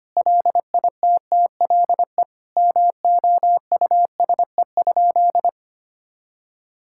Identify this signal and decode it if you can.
Morse code, heard as a keyed tone, about 25 wpm, 700 Hz. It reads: LITTLE MOUSE?
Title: Sentences from Top 1000 Words - Review Part 5 - 25wpm